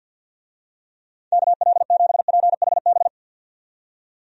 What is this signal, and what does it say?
Morse code, heard as a keyed tone, about 52 wpm, 700 Hz. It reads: KL7PHB
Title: DXCC-100 Call Signs - 6 Characters - 50wpm